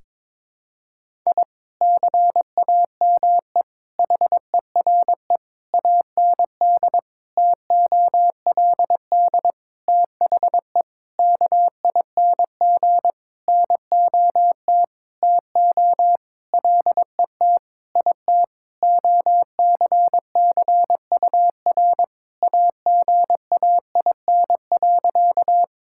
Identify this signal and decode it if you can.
Morse code, heard as a keyed tone, about 22 wpm, 700 Hz. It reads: I CAME HERE AND TOLD THE KING NOT TO LET IT OCCUR AGAIN.
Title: Sentences from Top 1000 Words - Review Part 6 - 22wpm